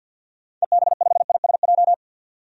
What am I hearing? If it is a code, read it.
Morse code, heard as a keyed tone, about 47 wpm, 700 Hz. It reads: EB5ISJ
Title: DXCC-100 Call Signs - 6 Characters - 50wpm